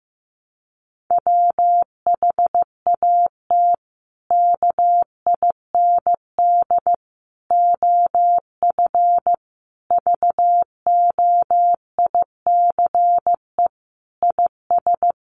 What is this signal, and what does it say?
Morse code, heard as a keyed tone, about 15 wpm, 700 Hz. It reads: WHAT KIND OF VOICE IS
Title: Sentences from Top 500 Words 15wpm